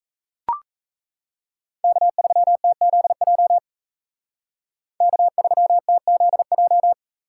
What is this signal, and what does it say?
Morse code, heard as a keyed tone, about 34 wpm, 700 Hz. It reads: K3TZJ K3TZJ
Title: Speed Racing - USA 1x3 Call Signs 28wpm